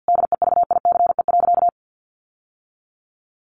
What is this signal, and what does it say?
Morse code, heard as a keyed tone, about 50 wpm, 700 Hz. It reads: DEVICE.
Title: Sentences from Top 1500 Words 50wpm